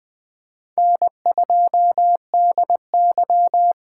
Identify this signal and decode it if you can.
Morse code, heard as a keyed tone, about 20 wpm, 700 Hz. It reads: N2DY
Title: USA Callsign DE Callsign - 4 Characters - 20wpm